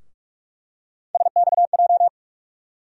Morse code, heard as a keyed tone, about 45 wpm, 700 Hz. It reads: SXJ
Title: USA Call Sign Postfixes - 3 Letters 45wpm